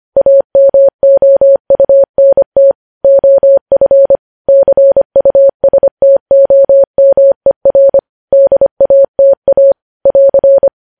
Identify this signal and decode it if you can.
Morse code, heard as a keyed tone, about 25 wpm, 550 Hz. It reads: AMOUNT OF CUSTOMER DATA <AR>
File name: podcast / News headlines-1x-25x25